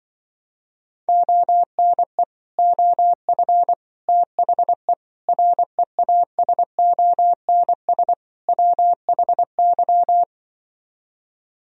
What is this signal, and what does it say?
Morse code, heard as a keyed tone, about 24 wpm, 700 Hz. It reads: ONE OF THE REASONS WHY
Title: Speed-Racing - Sets of 5 Words 20wpm